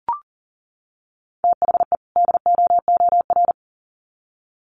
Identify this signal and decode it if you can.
Morse code, heard as a keyed tone, about 38 wpm, 700 Hz. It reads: THE DOOR
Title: Sets of 2 Words 40wpm